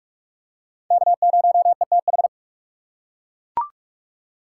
Morse code, heard as a keyed tone, about 40 wpm, 700 Hz. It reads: K0ETH